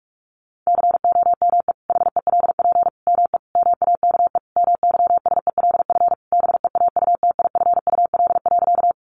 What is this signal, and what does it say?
Morse code, heard as a keyed tone, about 44 wpm, 700 Hz. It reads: COME HELP ME MAKE MYSELF BEAUTIFUL.
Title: Sentences from Top 1300 Words 45wpm